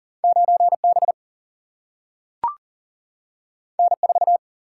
Morse code, heard as a keyed tone, about 40 wpm, 700 Hz. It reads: 9 B N 4
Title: Single-Letter Single-Number 40wpm